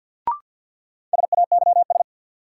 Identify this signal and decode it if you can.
Morse code, heard as a keyed tone, about 53 wpm, 700 Hz. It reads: SAYS
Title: Sets of 2 Words - New Word Drill - 50wpm